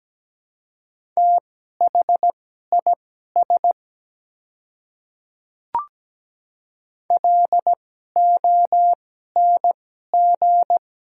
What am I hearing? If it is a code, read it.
Morse code, heard as a keyed tone, about 17 wpm, spaced as for 12 wpm, 700 Hz. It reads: THIS LONG
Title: 2x Character Spacing - Top 100 Words - 17wpm